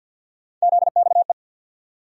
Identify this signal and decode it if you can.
Morse code, heard as a keyed tone, about 48 wpm, 700 Hz. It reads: ZXE
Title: USA Call Sign Postfixes - 3 Letters 50wpm